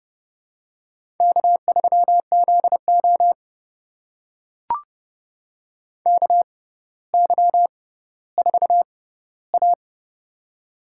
Morse code, heard as a keyed tone, about 30 wpm, spaced as for 15 wpm, 700 Hz. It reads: K3ZO KY4A
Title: USA Call Signs - 4 Characters - 6x Character Spacing - 30wpm